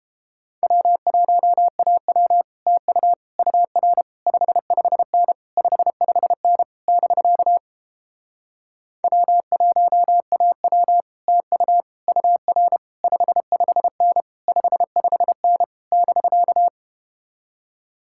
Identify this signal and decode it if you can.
Morse code, heard as a keyed tone, about 32 wpm, 700 Hz. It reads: W1AW TU UR 55N 55N <BK> W1AW TU UR 55N 55N <BK>